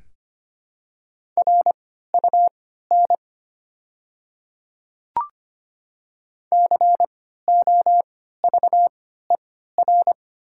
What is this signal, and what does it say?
Morse code, heard as a keyed tone, about 25 wpm, spaced as for 14 wpm, 700 Hz. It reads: RUN COVER
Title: 3x Character Spacing - Top 300 Words - 25wpm